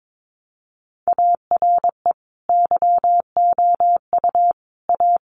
Morse code, heard as a keyed tone, about 22 wpm, 700 Hz. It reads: ARE YOU A
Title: Sentences from Top 1100 Words 22wpm